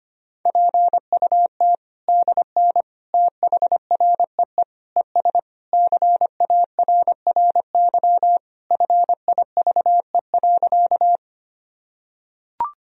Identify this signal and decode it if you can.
Morse code, heard as a keyed tone, about 25 wpm, 700 Hz. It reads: PUT DN THREE ES CARRY FIVE.
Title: Sentences from Top 400 Words and Abbreviations 25wpm